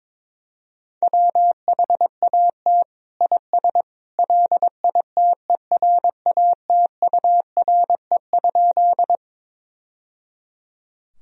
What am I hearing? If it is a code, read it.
Morse code, heard as a keyed tone, about 22 wpm, 700 Hz. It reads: WHAT IS LITERATURE?
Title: Sentences from Top 1000 Words - Review Part 3 - 22wpm